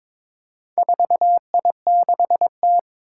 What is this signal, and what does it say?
Morse code, heard as a keyed tone, about 22 wpm, 700 Hz. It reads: 4I6T
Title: Four Challenging Letter-Number Combinations 22wpm